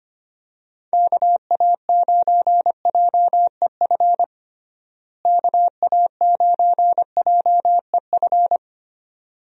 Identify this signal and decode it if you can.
Morse code, heard as a keyed tone, about 25 wpm, 700 Hz. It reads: KA9JEF KA9JEF
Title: USA Call Signs - 6 Characters - Repeated 2x - 25wpm